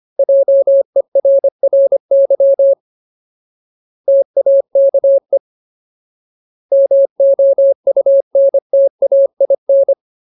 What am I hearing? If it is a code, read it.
Morse code, heard as a keyed tone, about 25 wpm, 550 Hz. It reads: JERRY TAKE MOUNTAIN
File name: Ultimate Word List / 25WPM Ultimate